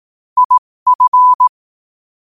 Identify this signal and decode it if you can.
Morse code, heard as a keyed tone, about 18 wpm, spaced as for 15 wpm, 1000 Hz. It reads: IF
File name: qotdfiles / 15wpm